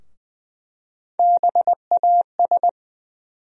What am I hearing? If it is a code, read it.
Morse code, heard as a keyed tone, about 20 wpm, 700 Hz. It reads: BAS
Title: USA Call Sign Postfixes - 3 Letters 20wpm